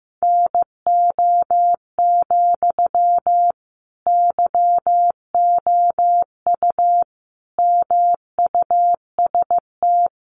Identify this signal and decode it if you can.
Morse code, heard as a keyed tone, about 15 wpm, 700 Hz. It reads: NO, YOU MUST
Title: Sentences from Top 1200 Words 15wpm